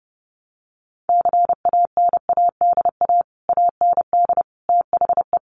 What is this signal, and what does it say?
Morse code, heard as a keyed tone, about 30 wpm, 700 Hz. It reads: CANADA AND THE
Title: Sets of 5 Words Encore 30wpm